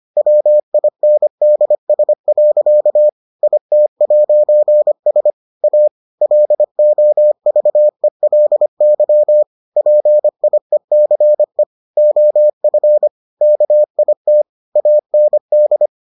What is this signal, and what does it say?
Morse code, heard as a keyed tone, about 25 wpm, 600 Hz. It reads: WINDS. IT'S A LOVELY PIECE OF KIT AND